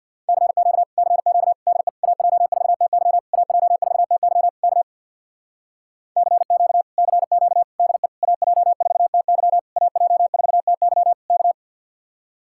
Morse code, heard as a keyed tone, about 64 wpm, 700 Hz. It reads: CQ CQ DE AJ4TY AJ4TY K CQ CQ DE AJ4TY AJ4TY K